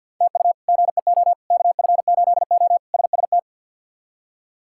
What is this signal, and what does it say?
Morse code, heard as a keyed tone, about 50 wpm, 700 Hz. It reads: TU GEO KF8O SST
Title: K1USN SST QSO Practice - 50wpm